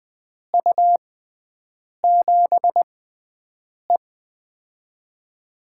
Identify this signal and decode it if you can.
Morse code, heard as a keyed tone, about 20 wpm, spaced as for 7 wpm, 700 Hz. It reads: U7E
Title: USA Call Signs - 4 Characters - 6x Character Spacing - 20wpm